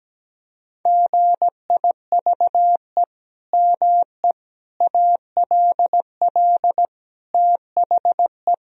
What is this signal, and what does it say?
Morse code, heard as a keyed tone, about 17 wpm, 700 Hz. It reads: GIVE ME ALL THE
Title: Sentences from Top 1000 Words - Review Part 3 - 17wpm